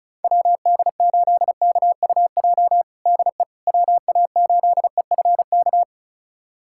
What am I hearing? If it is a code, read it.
Morse code, heard as a keyed tone, about 36 wpm, 700 Hz. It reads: WD8KUJ DE WA8EFK
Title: USA Callsign DE Callsign 35wpm